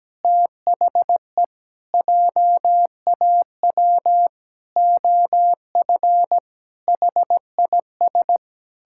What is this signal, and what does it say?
Morse code, heard as a keyed tone, about 17 wpm, 700 Hz. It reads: THE JAW OF HIS
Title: Pangram Challenge v2 - 17wpm